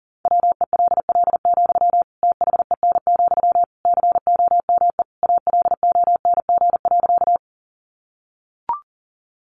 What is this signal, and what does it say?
Morse code, heard as a keyed tone, about 40 wpm, 700 Hz. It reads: WELL, THEN, COME ALONG.
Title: Sentences from Top 700 Words 40wpm